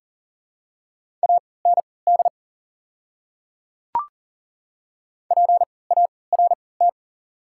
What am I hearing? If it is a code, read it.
Morse code, heard as a keyed tone, about 40 wpm, spaced as for 23 wpm, 700 Hz. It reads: AND PART
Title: 3x Character Spacing - Top 300 Words - 40wpm